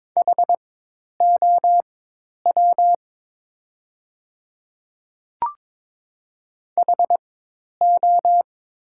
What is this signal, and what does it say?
Morse code, heard as a keyed tone, about 22 wpm, spaced as for 10 wpm, 700 Hz. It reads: HOW HO
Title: 4x Character Spacing - Top 200 Words - 22wpm